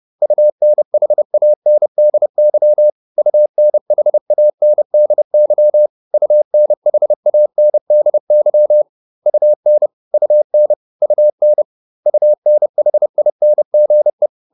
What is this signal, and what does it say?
Morse code, heard as a keyed tone, about 30 wpm, 600 Hz. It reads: UNHANDY UNHANDY UNHANDY UN UN UN UNHINGE